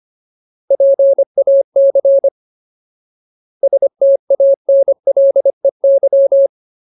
Morse code, heard as a keyed tone, about 25 wpm, 550 Hz. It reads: PAC STANLEY
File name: Ultimate Word List / 25WPM Ultimate